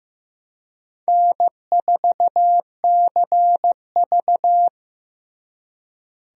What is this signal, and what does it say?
Morse code, heard as a keyed tone, about 15 wpm, 700 Hz. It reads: N4CV
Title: USA Call Signs - 4 Characters - Repeated 3x - 15wpm